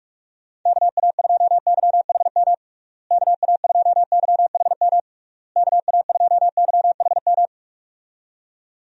Morse code, heard as a keyed tone, about 41 wpm, 700 Hz. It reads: KA2YHM KA2YHM KA2YHM
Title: USA Call Signs - 6 Characters - Repeated 3x - 45wpm